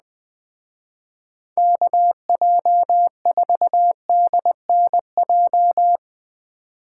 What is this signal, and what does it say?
Morse code, heard as a keyed tone, about 20 wpm, 700 Hz. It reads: KJ4DNJ